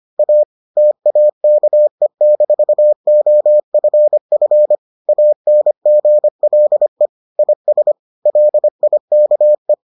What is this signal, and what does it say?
Morse code, heard as a keyed tone, about 25 wpm, 600 Hz. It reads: A TAKE-OFF ANGLE IS LIKE